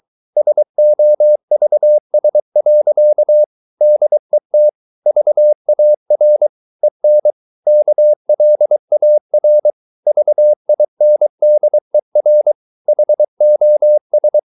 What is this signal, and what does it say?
Morse code, heard as a keyed tone, about 23 wpm, 600 Hz. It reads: SOVS. DET VAR EN KLAR VINDER HOS
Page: Ferie 23 wpm